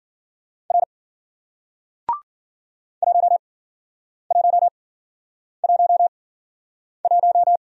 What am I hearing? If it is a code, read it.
Morse code, heard as a keyed tone, about 53 wpm, spaced as for 9 wpm, 700 Hz. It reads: R 1111